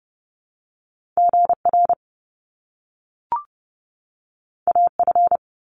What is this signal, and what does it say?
Morse code, heard as a keyed tone, about 30 wpm, 700 Hz. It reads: GR AF
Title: Top Two Letter Combinations in Words 30wpm